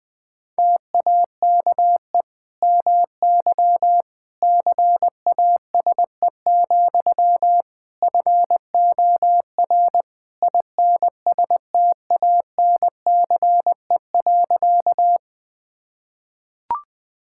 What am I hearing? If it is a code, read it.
Morse code, heard as a keyed tone, about 19 wpm, 700 Hz. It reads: TAKE MY CASE, FOR INSTANCE.
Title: Sentences from Top 1000 Words - Review Part 5 - 20wpm